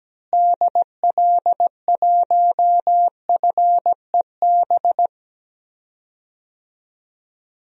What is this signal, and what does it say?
Morse code, heard as a keyed tone, about 17 wpm, 700 Hz. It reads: DL1FEB
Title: DXCC-100 Call Signs - 6 Characters - Repeated 3x - 17wpm